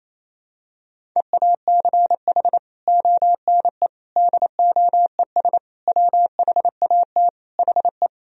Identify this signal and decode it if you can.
Morse code, heard as a keyed tone, about 27 wpm, 700 Hz. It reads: EACH ONE DOES WHAT HE
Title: Sentences from Top 1000 Words - Review Part 1 - 28wpm